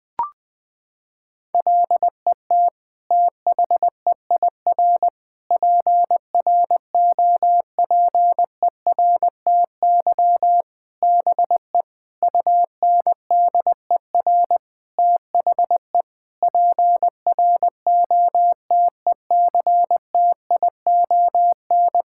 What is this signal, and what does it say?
Morse code, heard as a keyed tone, about 20 wpm, 700 Hz. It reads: LET THEIR PROPERTY BE UNDER THE PROTECTION
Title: Sentences from Top 1600 Words 20wpm